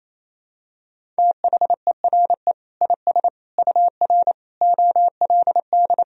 Morse code, heard as a keyed tone, about 28 wpm, 700 Hz. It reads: THERE IS UR OLD
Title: Sentences from Top 500 Words and Abbreviations 28wpm